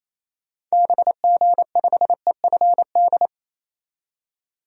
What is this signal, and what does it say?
Morse code, heard as a keyed tone, about 28 wpm, 700 Hz. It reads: BG5EFD
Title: DXCC-100 Call Signs - 6 Characters - 28wpm